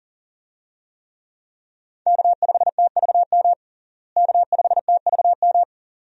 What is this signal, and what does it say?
Morse code, heard as a keyed tone, about 40 wpm, 700 Hz. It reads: K5TVM K5TVM
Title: USA Call Signs - 5 Characters - Repeated 2x - 40wpm